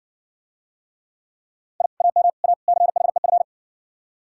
Morse code, heard as a keyed tone, about 60 wpm, 700 Hz. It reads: I AM A CHEF